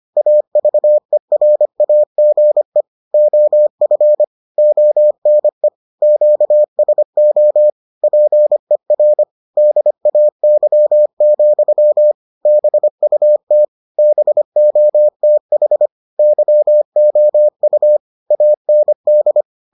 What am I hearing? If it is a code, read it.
Morse code, heard as a keyed tone, about 25 wpm, 600 Hz. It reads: AVERAGE OF ONE QSO PER DAY, BUT BOTH YOU AND